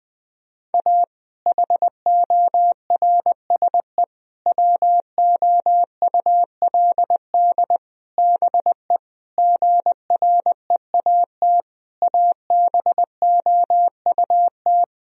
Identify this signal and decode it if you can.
Morse code, heard as a keyed tone, about 19 wpm, 700 Hz. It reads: A HORSE WOULD BE GREAT ABOUT
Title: Sentences from Top 1000 Words - Review Part 1 - 20wpm